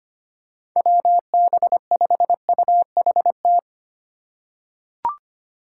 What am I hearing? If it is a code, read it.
Morse code, heard as a keyed tone, about 25 wpm, 700 Hz. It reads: WB5UHT
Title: USA Call Signs - 6 Characters - 2x Character Spacing - 25wpm